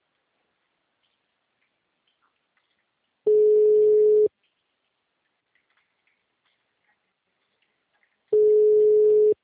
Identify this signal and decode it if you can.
Ringback tone (Europe, 425 Hz).
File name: out- -2021- - - .40523